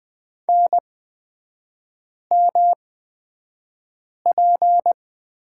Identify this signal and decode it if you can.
Morse code, heard as a keyed tone, about 20 wpm, spaced as for 5 wpm, 700 Hz. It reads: NMP
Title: 250 Lesson 40e - BK - Three Characters 20wpm